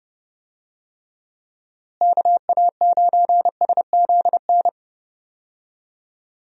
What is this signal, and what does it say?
Morse code, heard as a keyed tone, about 30 wpm, 700 Hz. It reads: KA9SZN
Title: USA Call Signs - 6 Characters - Repeated 2x - 30wpm